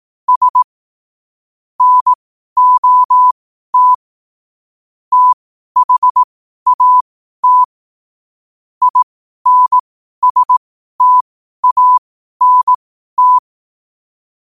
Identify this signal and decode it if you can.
Morse code, heard as a keyed tone, about 18 wpm, spaced as for 12 wpm, 1000 Hz. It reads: S NOT THAT INSTANT
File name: qotdfiles / 12wpm